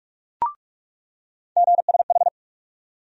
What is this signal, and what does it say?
Morse code, heard as a keyed tone, about 46 wpm, 700 Hz. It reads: GSH
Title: USA Call Sign Postfixes - 3 Letters 45wpm